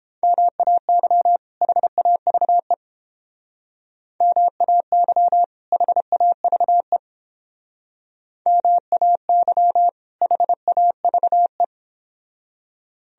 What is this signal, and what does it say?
Morse code, heard as a keyed tone, about 30 wpm, 700 Hz. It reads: MAY HAVE MAY HAVE MAY HAVE